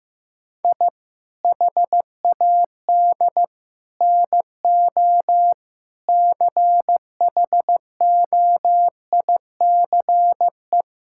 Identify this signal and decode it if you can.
Morse code, heard as a keyed tone, about 15 wpm, 700 Hz. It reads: I HAD NO CHOICE
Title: Sentences from Top 1000 Words - Review Part 6 - 15wpm